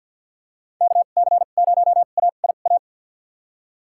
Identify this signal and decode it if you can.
Morse code, heard as a keyed tone, about 53 wpm, spaced as for 37 wpm, 700 Hz. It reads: KC0AIA
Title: USA Call Signs - 2x Character Spacing - 50wpm